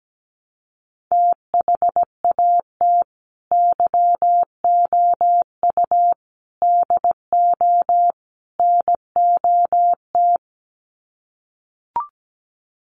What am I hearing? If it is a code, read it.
Morse code, heard as a keyed tone, about 17 wpm, 700 Hz. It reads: THAT YOU DO NOT
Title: Sets of 4 Words 17wpm